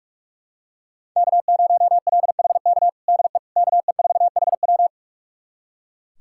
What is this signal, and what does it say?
Morse code, heard as a keyed tone, about 42 wpm, 700 Hz. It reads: K0LHK DE KE4HW
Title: USA Callsign DE Callsign - 5 Characters - 45wpm